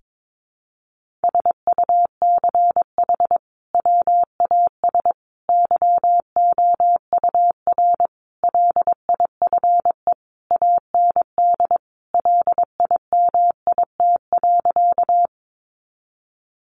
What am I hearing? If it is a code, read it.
Morse code, heard as a keyed tone, about 22 wpm, 700 Hz. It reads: SUCH WAS YOUR LIFE AND LIMIT.
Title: Sentences from Top 1500 Words 22wpm